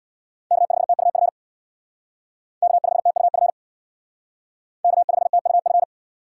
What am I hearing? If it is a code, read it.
Morse code, heard as a keyed tone, about 62 wpm, 700 Hz. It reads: K5TUF K5TUF K5TUF
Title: Speed Racing - USA 1x3 Call Signs 50wpm